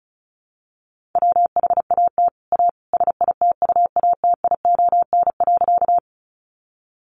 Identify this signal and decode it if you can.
Morse code, heard as a keyed tone, about 34 wpm, 700 Hz. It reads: WHAT A SITUATION.
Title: Sentences from Top 1200 Words 35wpm